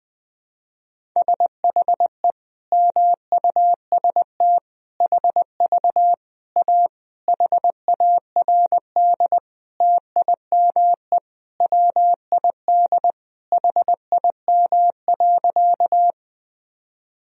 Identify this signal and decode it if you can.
Morse code, heard as a keyed tone, about 20 wpm, 700 Hz. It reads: SHE MUST HV A HARD TIME WID HIM.
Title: Sentences from Top 300 Words and Abbreviations 20wpm